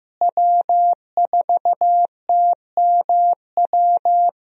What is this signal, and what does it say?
Morse code, heard as a keyed tone, about 15 wpm, 700 Hz. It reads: W4TMW
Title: USA Call Signs - 5 Characters - Repeated 2x - 15wpm